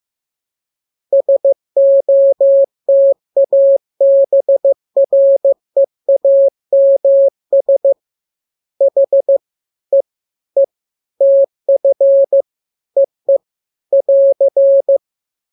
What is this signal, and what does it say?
Morse code, heard as a keyed tone, about 15 wpm, 550 Hz. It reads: SOTA BREAMS H E E TF EE <AR>